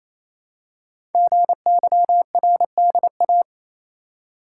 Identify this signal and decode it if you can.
Morse code, heard as a keyed tone, about 29 wpm, 700 Hz. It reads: GYRDA
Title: Code Groups - 5 Letters - 28wpm